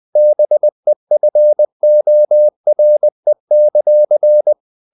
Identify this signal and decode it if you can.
Morse code, heard as a keyed tone, about 20 wpm, 600 Hz. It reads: BEFORE;
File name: the old man and the sea / the old man and the sea part 1 20wpm